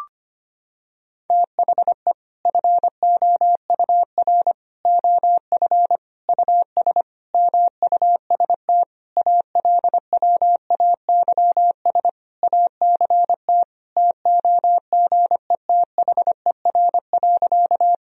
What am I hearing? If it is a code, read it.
Morse code, heard as a keyed tone, about 25 wpm, 700 Hz. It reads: THE FOUR OF US MUST ALWAYS ACT TOGETHER.
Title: Sentences from Top 1000 Words - Review Part 1 - 25wpm